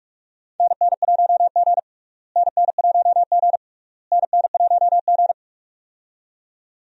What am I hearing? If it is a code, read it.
Morse code, heard as a keyed tone, about 46 wpm, 700 Hz. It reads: NN1G NN1G NN1G
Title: USA Call Signs - 4 Characters - Repeated 3x - 45wpm